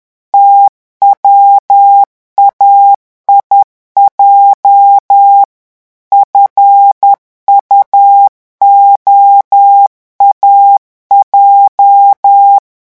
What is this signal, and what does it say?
Morse code, heard as a keyed tone, about 10 wpm, 800 Hz. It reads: TWAIJ FUOAJ
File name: morse mp3 / revision8-vitesse 10 mots